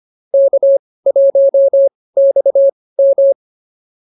Morse code, heard as a keyed tone, about 24 wpm, spaced as for 18 wpm, 550 Hz. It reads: K1XM